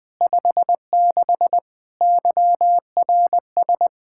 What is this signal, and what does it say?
Morse code, heard as a keyed tone, about 20 wpm, 700 Hz. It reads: 56 YRS
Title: Practice QSO - Exchange 3 - More Info - 20wpm